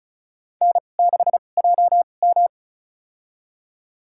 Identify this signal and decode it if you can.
Morse code, heard as a keyed tone, about 36 wpm, spaced as for 25 wpm, 700 Hz. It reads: N6JM
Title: USA Call Signs - 2x Character Spacing - 35wpm